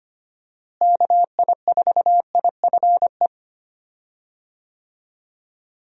Morse code, heard as a keyed tone, about 25 wpm, 700 Hz. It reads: KI4IFE
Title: USA Call Signs - 6 Characters - Repeated 3x - 25wpm